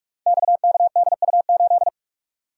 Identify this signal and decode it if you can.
Morse code, heard as a keyed tone, about 44 wpm, 700 Hz. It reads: XKDU8
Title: Code Groups - 5 Letter:Numbers - 45wpm